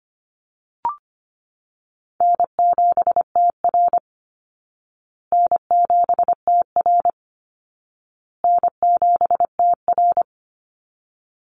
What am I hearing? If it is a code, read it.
Morse code, heard as a keyed tone, about 25 wpm, 700 Hz. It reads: N7TR N7TR N7TR
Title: Call Signs Repeated 2x - 25wpm